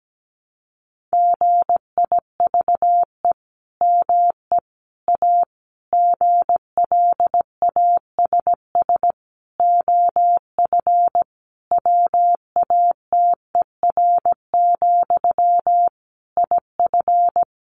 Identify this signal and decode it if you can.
Morse code, heard as a keyed tone, about 17 wpm, 700 Hz. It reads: GIVE ME A GLASS OF WATER, IF
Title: Sentences from Top 1500 Words 17wpm